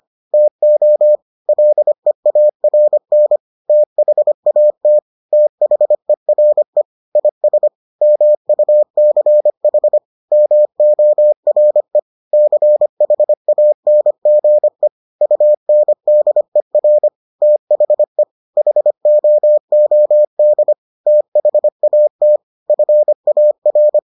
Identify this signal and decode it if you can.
Morse code, heard as a keyed tone, about 25 wpm, 600 Hz. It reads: TO LEARN THAT THERE IS MUCH MORE CHANGE UNDER THE HOOD THAT FAR